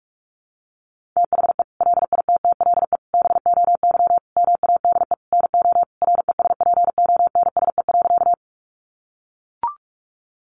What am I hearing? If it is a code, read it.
Morse code, heard as a keyed tone, about 44 wpm, 700 Hz. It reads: THE LITTLE BOY MADE NO RESPONSE.
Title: Sentences from Top 1400 Words 45wpm